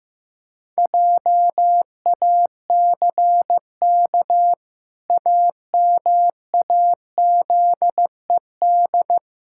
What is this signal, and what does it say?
Morse code, heard as a keyed tone, about 15 wpm, 700 Hz. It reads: JACK AMAZED
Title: Pangram Challenge v2 - 15wpm